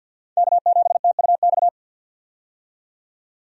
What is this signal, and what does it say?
Morse code, heard as a keyed tone, about 52 wpm, 700 Hz. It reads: K7TUX
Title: USA Call Signs - 5 Characters - 50wpm